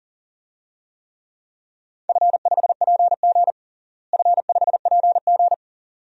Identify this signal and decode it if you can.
Morse code, heard as a keyed tone, about 40 wpm, 700 Hz. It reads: F5PG F5PG
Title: DXCC-100 Call Signs - 4 Characters - Repeated 2x - 40wpm